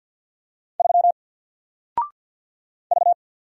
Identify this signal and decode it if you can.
Morse code, heard as a keyed tone, about 48 wpm, 700 Hz. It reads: 3 V
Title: Single Letter-Number 50wpm